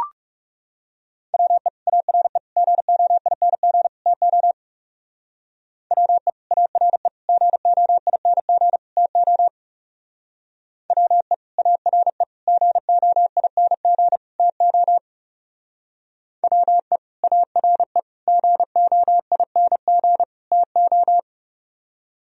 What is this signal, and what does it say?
Morse code, heard as a keyed tone, about 36 wpm, 700 Hz. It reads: WE ARE GOING TO WE ARE GOING TO WE ARE GOING TO WE ARE GOING TO
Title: Speed-Racing - Sets of 4 Words 30wpm